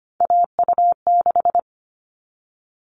Morse code, heard as a keyed tone, about 25 wpm, 700 Hz. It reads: AU6
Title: All Cut Numbers - 3-Digit 25wpm